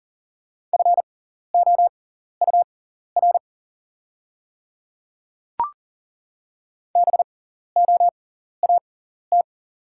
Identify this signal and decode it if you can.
Morse code, heard as a keyed tone, about 40 wpm, spaced as for 14 wpm, 690 Hz. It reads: FOUR BOAT
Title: 6x Character Spacing - Top 500 Words - 40wpm